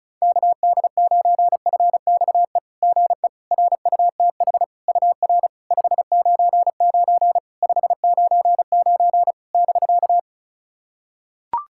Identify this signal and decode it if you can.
Morse code, heard as a keyed tone, about 36 wpm, 690 Hz. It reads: KD9FXE GE RUTH UR 599 599 <BK>
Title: SOTA QSO Practice 35wpm